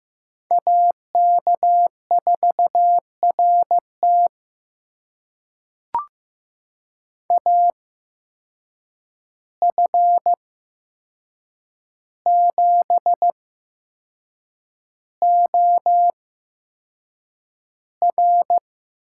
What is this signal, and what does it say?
Morse code, heard as a keyed tone, about 15 wpm, spaced as for 6 wpm, 700 Hz. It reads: AK4RT AF7OR
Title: USA Call Signs - 8x Character Spacing - 15wpm